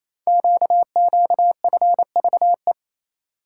Morse code, heard as a keyed tone, about 27 wpm, 700 Hz. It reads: QQFVE
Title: Code Groups - 5 Letter:Numbers - 28wpm